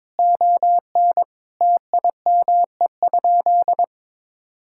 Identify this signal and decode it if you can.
Morse code, heard as a keyed tone, about 22 wpm, 700 Hz. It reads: ON TIME?
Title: Sentences from Top 1000 Words - Review Part 6 - 22wpm